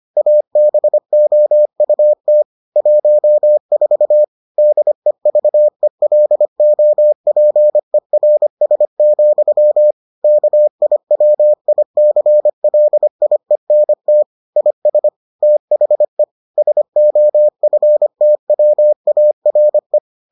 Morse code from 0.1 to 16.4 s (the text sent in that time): ABOUT 14 DEVELOPERS, KIWICLIENT IS THE